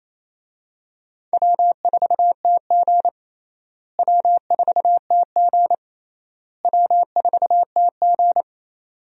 Morse code, heard as a keyed tone, about 28 wpm, 700 Hz. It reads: W4TG W4TG W4TG
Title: USA Call Signs - 4 Characters - Repeated 3x - 28wpm